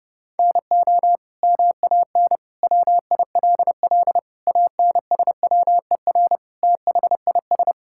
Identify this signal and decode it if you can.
Morse code, heard as a keyed tone, about 30 wpm, 700 Hz. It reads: NO MAN WILL ANSWER THIS
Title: Sentences from Top 1000 Words - Review Part 6 - 30wpm